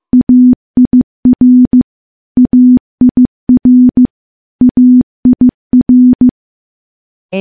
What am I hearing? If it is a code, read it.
Morse code, heard as a keyed tone, about 15 wpm, 250 Hz. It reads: AIR AIR AIR
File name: morse / morse-training-15WPM